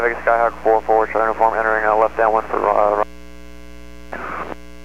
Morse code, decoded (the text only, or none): none